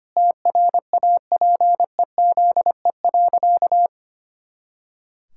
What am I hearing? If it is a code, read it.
Morse code, heard as a keyed tone, about 25 wpm, 700 Hz. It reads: TRAPEZE.
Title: Pangram Challenge v2 - 25wpm